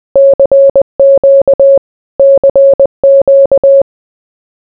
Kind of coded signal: Morse code